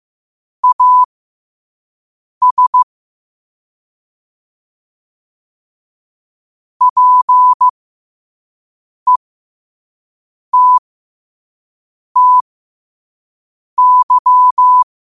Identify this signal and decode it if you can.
Morse code, heard as a keyed tone, about 15 wpm, spaced as for 5 wpm, 1000 Hz. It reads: AS PETTY
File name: qotdfiles / 5wpm